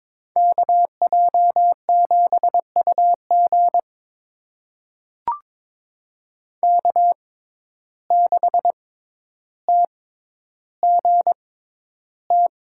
Morse code, heard as a keyed tone, about 22 wpm, spaced as for 11 wpm, 700 Hz. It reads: KJ7UG K6TGT
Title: USA Call Signs - 5 Characters - 6x Character Spacing - 22wpm